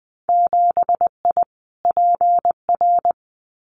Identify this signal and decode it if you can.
Morse code, heard as a keyed tone, about 20 wpm, 700 Hz. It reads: 7I PR
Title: Winter Field Day Exchanges Encore 20wpm